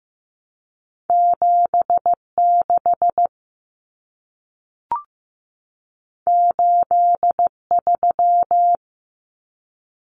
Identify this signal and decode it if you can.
Morse code, heard as a keyed tone, about 15 wpm, 700 Hz. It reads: 76 83
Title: Numbers - 2-Digit 15wpm